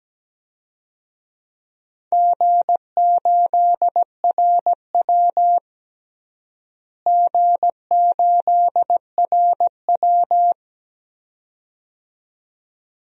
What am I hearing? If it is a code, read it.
Morse code, heard as a keyed tone, about 17 wpm, 700 Hz. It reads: G8RW G8RW
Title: DXCC-100 Call Signs - 4 Characters - Repeated 2x - 17wpm